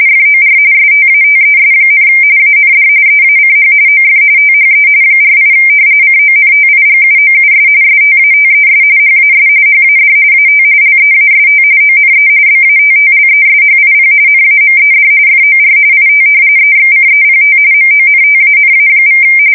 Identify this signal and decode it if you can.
RTTY, about 45.45 baud. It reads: FOR STATES, AND EACH OF THE 48 CONTIGUOUS STATES HAS HAD AT LEAST ONE BATTLESHIP NAMED FOR IT EXCEPT MONTANA=
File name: fsk170 / battleships